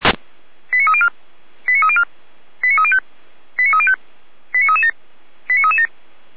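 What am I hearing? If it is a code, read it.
Five-tone selective call (ZVEI), heard as quick runs of five tones, tones of about 70 ms. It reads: 89304 89304 89305 89305 89337 89337
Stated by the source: ZVEI five-tone call received by SDR, from a server of feuerwehr-orlamuende.org (fire brigade Orlamünde).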